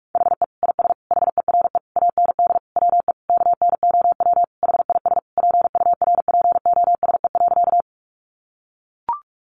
Morse code, heard as a keyed tone, about 44 wpm, 700 Hz. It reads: HE IS HERE AND WE KNOW HIS PURPOSE.
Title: Sentences from Top 800 Words 45wpm